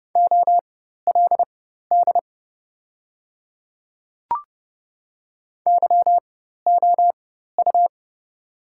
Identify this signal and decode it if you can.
Morse code, heard as a keyed tone, about 30 wpm, spaced as for 14 wpm, 690 Hz. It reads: OLD YOU
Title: 4x Character Spacing - Top 200 Words - 30wpm